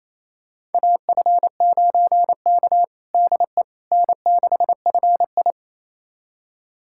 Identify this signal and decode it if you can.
Morse code, heard as a keyed tone, about 28 wpm, 700 Hz. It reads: AF9K DE N6FI
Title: USA Callsign DE Callsign - 4 Characters - 28wpm